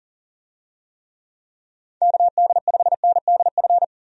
Morse code, heard as a keyed tone, about 39 wpm, 700 Hz. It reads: KD5NDF
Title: USA Call Signs - 6 Characters - 40wpm